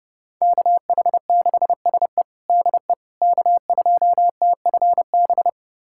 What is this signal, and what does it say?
Morse code, heard as a keyed tone, about 30 wpm, 700 Hz. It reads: KH6SE DE K2TFB
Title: USA Callsign DE Callsign - 5 Characters - 30wpm